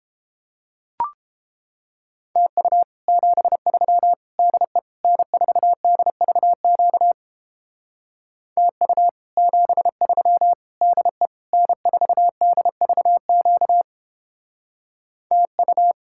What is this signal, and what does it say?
Morse code, heard as a keyed tone, about 29 wpm, 700 Hz. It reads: TU 73 DE N4DVQ TU 73 DE N4DVQ TU